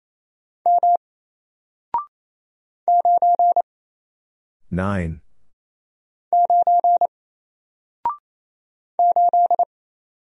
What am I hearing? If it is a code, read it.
Morse code, heard as a keyed tone, about 27 wpm, 700 Hz. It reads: M 9 9 8
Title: Single Letter-Number Alt 28wpm